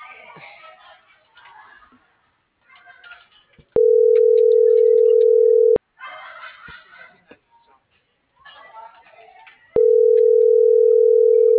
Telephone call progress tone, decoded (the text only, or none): ringback tone